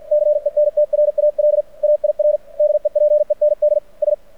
Morse code, heard as a keyed tone, about 51 wpm, spaced as for 44 wpm, 600 Hz. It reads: CENTRAL AIR DEPENDS
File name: archived / CRSnet